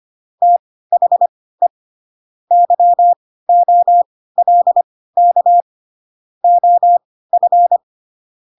Morse code, heard as a keyed tone, about 25 wpm, spaced as for 16 wpm, 700 Hz. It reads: THE YOLK OF